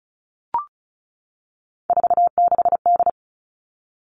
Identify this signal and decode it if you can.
Morse code, heard as a keyed tone, about 36 wpm, 700 Hz. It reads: 46D
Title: Three Challenging Letter-Number Combinations 35wpm